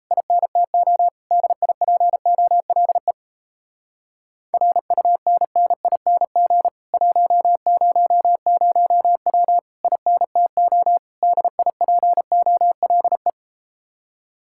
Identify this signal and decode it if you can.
Morse code, heard as a keyed tone, about 37 wpm, 700 Hz. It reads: INTO DIPOLE RUNNING 100W INTO DIPOLE